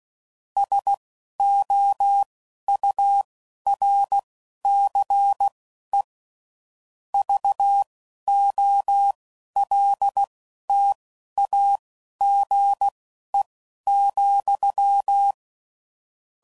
Morse code, heard as a keyed tone, about 16 wpm, spaced as for 11 wpm, 800 Hz. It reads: SOURCE VOLTAGE,
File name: CW Code Groups @10WPM / Technical Subjects / XC